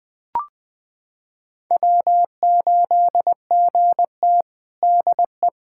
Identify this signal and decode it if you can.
Morse code, heard as a keyed tone, about 20 wpm, 700 Hz. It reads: W8GT DE
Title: USA Callsign DE Callsign - 4 Characters - 20wpm